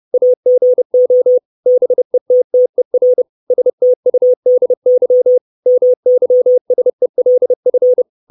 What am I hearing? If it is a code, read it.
Morse code, heard as a keyed tone, about 30 wpm, 500 Hz. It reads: AGO BETTER STUDY MYSELF
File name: 632 Words 30WPM